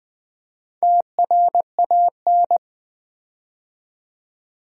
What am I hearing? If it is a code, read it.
Morse code, heard as a keyed tone, about 20 wpm, 700 Hz. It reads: TRAN
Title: Build A Word - 4 Letters At A Time - 10-Letter Words - 20wpm